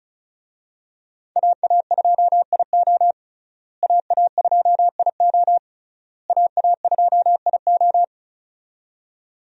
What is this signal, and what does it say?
Morse code, heard as a keyed tone, about 36 wpm, 700 Hz. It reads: AA2IO AA2IO AA2IO
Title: USA Call Signs - 5 Characters - Repeated 3x - 35wpm